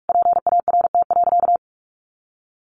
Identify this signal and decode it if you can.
Morse code, heard as a keyed tone, about 44 wpm, 700 Hz. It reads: PART.
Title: Sentences from Top 1200 Words 45wpm